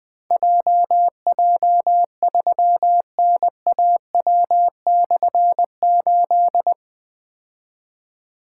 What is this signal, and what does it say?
Morse code, heard as a keyed tone, about 20 wpm, 700 Hz. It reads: JJ3NAW/8
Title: Difficult Call Signs 20wpm